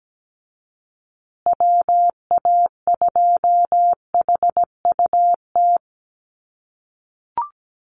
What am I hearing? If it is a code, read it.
Morse code, heard as a keyed tone, about 17 wpm, 700 Hz. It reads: WA2HUT
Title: USA Call Signs Repeated 2x 17wpm